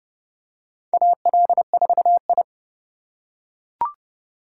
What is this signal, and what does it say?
Morse code, heard as a keyed tone, about 30 wpm, 700 Hz. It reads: AL4I
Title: USA Call Signs - 4 Characters - 4x Character Spacing - 30wpm